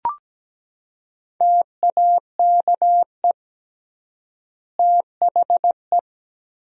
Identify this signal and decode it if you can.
Morse code, heard as a keyed tone, about 17 wpm, 700 Hz. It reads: TAKE THE
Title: 3x Word-Spacing - Sentences from Top 400 Words - 17wpm